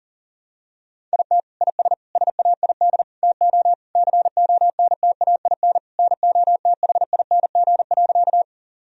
Morse code, heard as a keyed tone, about 40 wpm, 700 Hz. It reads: IT IS SAID TO CONTAIN NOTHING.
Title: Sentences from Top 1000 Words - Review Part 2 - 40wpm